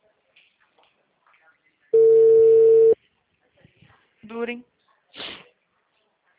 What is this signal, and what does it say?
Ringback tone (China, 450 Hz), with speech.
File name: out- -2021- - - .29685